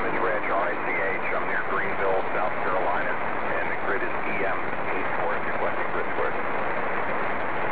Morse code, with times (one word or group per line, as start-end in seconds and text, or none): none